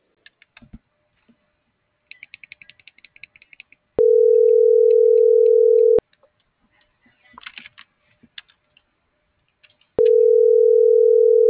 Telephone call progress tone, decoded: ringback tone

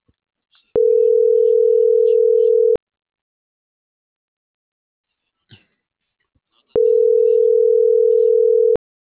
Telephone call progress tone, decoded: ringback tone

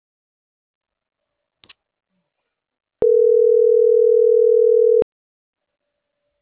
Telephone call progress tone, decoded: ringback tone

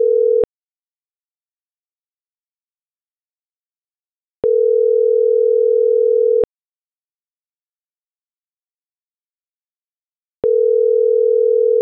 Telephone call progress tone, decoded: ringback tone

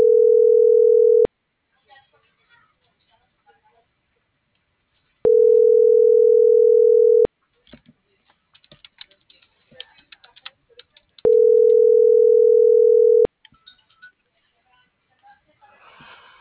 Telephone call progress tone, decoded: ringback tone

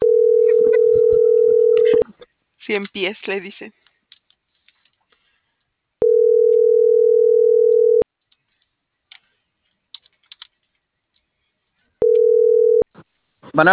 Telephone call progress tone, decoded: ringback tone

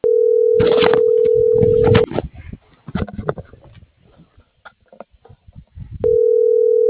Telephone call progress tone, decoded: ringback tone

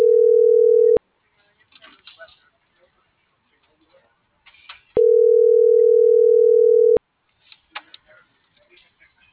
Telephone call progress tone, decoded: ringback tone

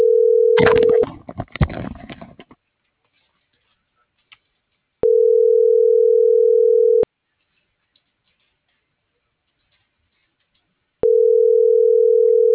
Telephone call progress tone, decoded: ringback tone

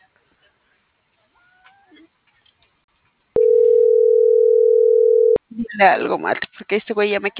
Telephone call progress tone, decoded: ringback tone

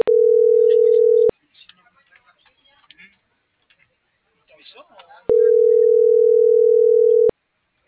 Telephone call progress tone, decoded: ringback tone

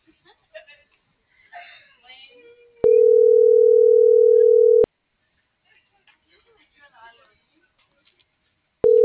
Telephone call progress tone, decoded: ringback tone

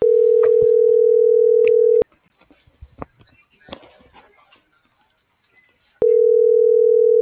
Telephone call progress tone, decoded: ringback tone

